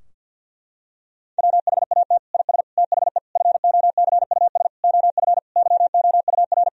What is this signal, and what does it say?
Morse code, heard as a keyed tone, about 50 wpm, 700 Hz. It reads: WHAT IS THE FOCUS OF YOUR
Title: Sentences from Top 1000 Words - Review Part 4 - 50wpm